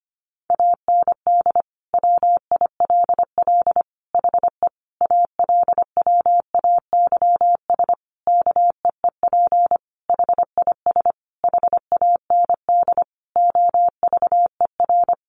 Morse code, heard as a keyed tone, about 25 wpm, 700 Hz. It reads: AND WILL HE ALWAYS KEEP HIS HAND OVER
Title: Sentences from Top 1200 Words 25wpm